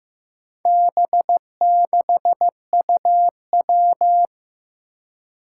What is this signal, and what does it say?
Morse code, heard as a keyed tone, about 15 wpm, 700 Hz. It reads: B6UW
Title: Four Challenging Letter-Number Combinations 15wpm